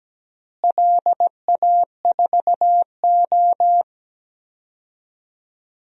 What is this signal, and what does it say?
Morse code, heard as a keyed tone, about 17 wpm, 700 Hz. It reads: LA4O
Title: DXCC-100 Call Signs - 4 Characters - Repeated 3x - 17wpm